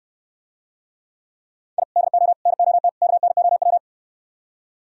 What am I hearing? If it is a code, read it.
Morse code, heard as a keyed tone, about 78 wpm, 700 Hz. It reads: I DO NOT KNOW